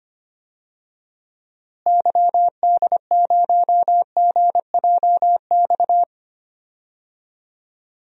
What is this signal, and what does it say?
Morse code, heard as a keyed tone, about 25 wpm, 700 Hz. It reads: YD0GJX
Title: DXCC-100 Call Signs - 6 Characters - Repeated 2x - 25wpm